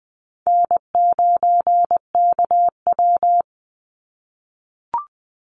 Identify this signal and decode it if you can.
Morse code, heard as a keyed tone, about 20 wpm, 700 Hz. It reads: N9KW
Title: NA SOTA Chaser Callsigns 20wpm